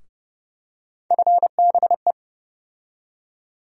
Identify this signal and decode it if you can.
Morse code, heard as a keyed tone, about 30 wpm, 700 Hz. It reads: FBE
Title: USA Call Sign Postfixes - 3 Letters 30wpm